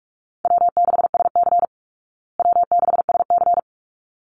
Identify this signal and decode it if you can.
Morse code, heard as a keyed tone, about 44 wpm, 700 Hz. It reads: W6SC W6SC
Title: Easy Call Signs Repeated 2x - 45wpm